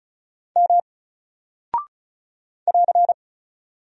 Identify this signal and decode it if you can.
Morse code, heard as a keyed tone, about 36 wpm, 700 Hz. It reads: M <AR>
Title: Single Character (Letter:Number:Punctuation:Prosign) 35wpm